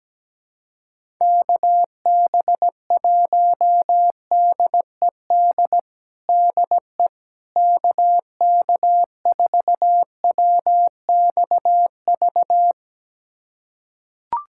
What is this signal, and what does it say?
Morse code, heard as a keyed tone, about 17 wpm, 700 Hz. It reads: KB1DED DE KK4WXV
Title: USA Callsign DE Callsign - 6 Characters - 17wpm